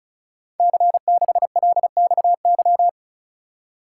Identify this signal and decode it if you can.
Morse code, heard as a keyed tone, about 35 wpm, 700 Hz. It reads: C6LXY
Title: Code Groups - 5 Letter:Numbers - 35wpm